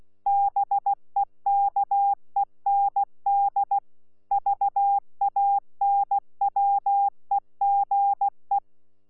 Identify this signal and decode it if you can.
Morse code, heard as a keyed tone, about 16 wpm, 800 Hz. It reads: BEKEND VANWEGE
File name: Z PI4VRZ / PI4VRZ 14-jun-2025